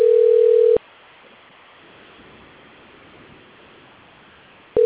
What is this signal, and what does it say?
Ringback tone (North America), 440 + 480 Hz.